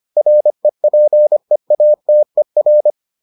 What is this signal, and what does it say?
Morse code, heard as a keyed tone, about 25 wpm, 600 Hz. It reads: REPEATER